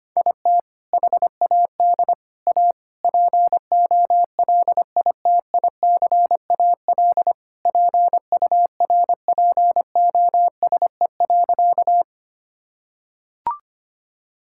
Morse code, heard as a keyed tone, about 25 wpm, 700 Hz. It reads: IT HAD A POLITICAL PURPOSE.
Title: Sentences from Top 1000 Words - Review Part 5 - 25wpm